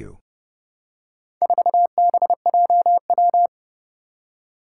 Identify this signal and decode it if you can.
Morse code, heard as a keyed tone, about 30 wpm, 700 Hz. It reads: 4BJW
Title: Four Challenging Letter-Number Combinations 30wpm